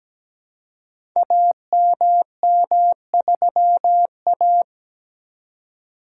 Morse code, heard as a keyed tone, about 17 wpm, 700 Hz. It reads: AMM3A
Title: Code Groups - 5 Letter:Numbers - 17wpm